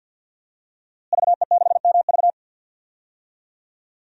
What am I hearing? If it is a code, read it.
Morse code, heard as a keyed tone, about 52 wpm, 700 Hz. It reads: VE6MV
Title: DXCC-100 Call Signs - 5 Characters - Repeated 3x - 50wpm